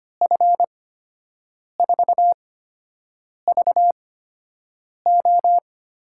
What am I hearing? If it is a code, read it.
Morse code, heard as a keyed tone, about 25 wpm, spaced as for 7 wpm, 700 Hz. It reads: F4VO
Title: USA Call Signs - 6 Characters - 8x Character Spacing - 25wpm